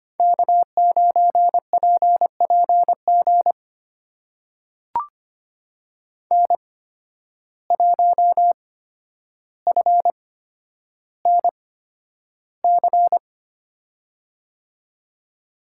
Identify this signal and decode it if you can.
Morse code, heard as a keyed tone, about 25 wpm, spaced as for 11 wpm, 700 Hz. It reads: K9PPG N1FNC
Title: USA Call Signs - 5 Characters - 8x Character Spacing - 25wpm